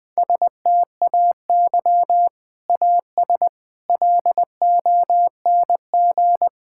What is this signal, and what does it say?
Morse code, heard as a keyed tone, about 20 wpm, 700 Hz. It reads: STAY AS LONG
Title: Sentences with 3-Word Phrases - Spoken Beginning - 20wpm